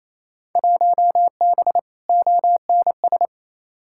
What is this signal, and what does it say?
Morse code, heard as a keyed tone, about 28 wpm, 700 Hz. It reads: 1B ONS
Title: Field Day Exchanges Encore - 28wpm